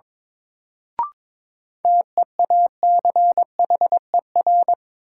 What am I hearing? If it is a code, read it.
Morse code, heard as a keyed tone, about 22 wpm, 700 Hz. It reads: TEACHER
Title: Sentences from Top 900 Words - New Word Drill - 22wpm